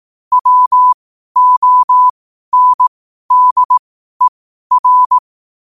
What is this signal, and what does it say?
Morse code, heard as a keyed tone, about 18 wpm, spaced as for 12 wpm, 1000 Hz. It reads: WONDER
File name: qotdfiles / 12wpm